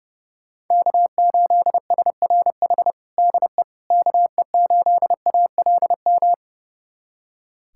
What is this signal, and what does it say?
Morse code, heard as a keyed tone, about 32 wpm, 700 Hz. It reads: K8SRH DE KE8ALM
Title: USA Callsign DE Callsign 30wpm